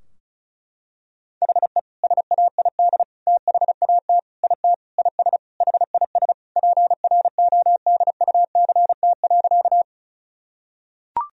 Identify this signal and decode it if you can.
Morse code, heard as a keyed tone, about 34 wpm, 700 Hz. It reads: HE SAID THAT IT IS HIS PRODUCT.
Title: Sentences from Top 1000 Words - Review Part 2 - 35wpm